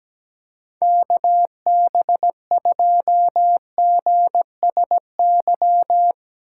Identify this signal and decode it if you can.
Morse code, heard as a keyed tone, about 21 wpm, 700 Hz. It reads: KB2GSY
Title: Speed Racing - USA 2x3 Call Signs 17wpm